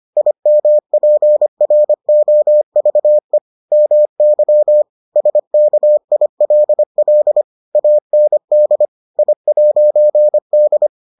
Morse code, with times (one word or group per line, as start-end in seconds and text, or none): 0.2-3.4 IMPROVE
3.7-4.8 MY
5.2-7.4 SKILL
7.8-8.9 AND
9.2-10.9 I'D